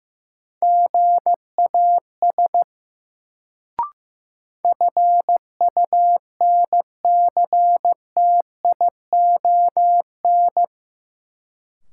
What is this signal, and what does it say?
Morse code, heard as a keyed tone, about 15 wpm, 700 Hz. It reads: GAS FUNCTION
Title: Sentences from Top 1100 Words - New Word Drill - 15wpm